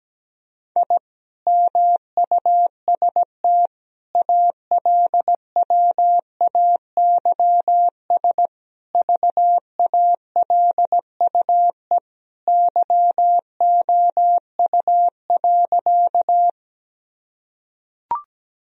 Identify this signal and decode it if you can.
Morse code, heard as a keyed tone, about 17 wpm, 700 Hz. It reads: I MUST ALWAYS VALUE YOU.
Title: Sentences from Top 1000 Words - Review Part 3 - 17wpm